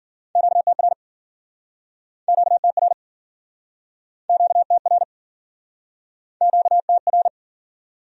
Morse code, heard as a keyed tone, about 56 wpm, 700 Hz. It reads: QTR QTR QTR QTR